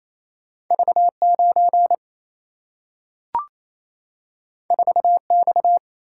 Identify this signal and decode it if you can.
Morse code, heard as a keyed tone, about 28 wpm, 700 Hz. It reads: V 9 4 X
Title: Single-Letter Single-Number 28wpm